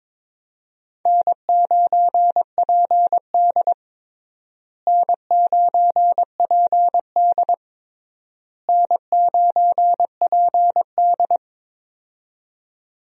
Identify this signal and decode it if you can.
Morse code, heard as a keyed tone, about 22 wpm, 700 Hz. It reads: N9PD N9PD N9PD
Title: USA Call Signs - 4 Characters - Repeated 3x - 22wpm